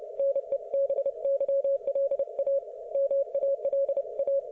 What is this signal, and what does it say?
Morse code, heard as a keyed tone, about 28 wpm, 550 Hz. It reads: NEBYLA MALA
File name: Archiv / CUC181 / ok1fhi cul181